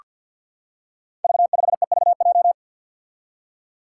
Morse code, heard as a keyed tone, about 48 wpm, 700 Hz. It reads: V5EVJ
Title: Five Challenging Letter-Number Combinations 50wpm